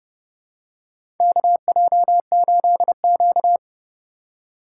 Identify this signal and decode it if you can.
Morse code, heard as a keyed tone, about 30 wpm, 700 Hz. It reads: KJ8Q
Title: USA Call Signs - 3x Character Spacing - 30wpm